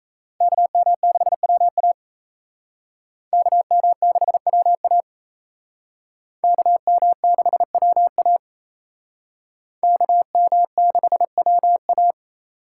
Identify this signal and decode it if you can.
Morse code, heard as a keyed tone, about 35 wpm, 700 Hz. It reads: KM6WA KM6WA KM6WA KM6WA
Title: Speed Racing - USA 2x2 Call Signs 28wpm